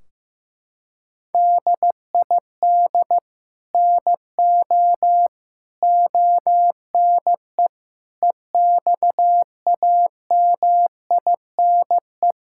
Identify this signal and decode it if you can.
Morse code, heard as a keyed tone, about 15 wpm, 700 Hz. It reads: DID NO ONE EXAMINE
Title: Sentences from Top 1600 Words 15wpm